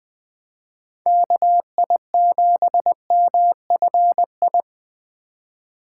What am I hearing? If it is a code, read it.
Morse code, heard as a keyed tone, about 20 wpm, 700 Hz. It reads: KI7MFI
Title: Speed Racing - USA 2x3 Call Signs 17wpm